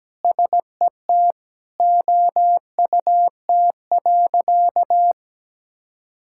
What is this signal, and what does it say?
Morse code, heard as a keyed tone, about 17 wpm, 700 Hz. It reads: SET OUT.
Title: Sentences from Top 1600 Words 17wpm